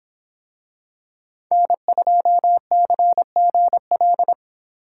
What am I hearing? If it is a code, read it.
Morse code, heard as a keyed tone, about 24 wpm, 700 Hz. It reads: N2CGL